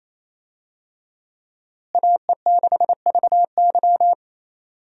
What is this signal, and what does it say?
Morse code, heard as a keyed tone, about 28 wpm, 700 Hz. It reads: AE6VY
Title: USA Call Signs - 5 Characters - Repeated 3x - 28wpm